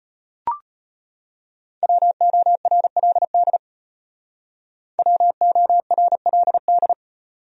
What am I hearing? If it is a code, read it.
Morse code, heard as a keyed tone, about 31 wpm, 700 Hz. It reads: WORLD WORLD
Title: Speed-Racing Top 500 Words 25wpm